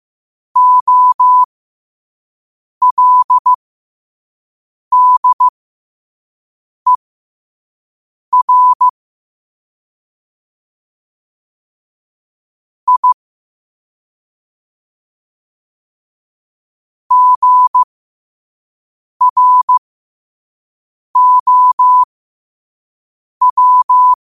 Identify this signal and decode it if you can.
Morse code, heard as a keyed tone, about 15 wpm, spaced as for 5 wpm, 1000 Hz. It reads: OLDER I GROW